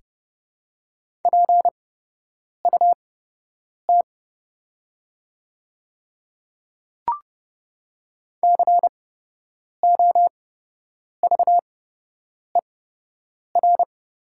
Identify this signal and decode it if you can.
Morse code, heard as a keyed tone, about 30 wpm, spaced as for 8 wpm, 700 Hz. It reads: PUT COVER
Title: 8x Character Spacing - Top 300 Words - 30wpm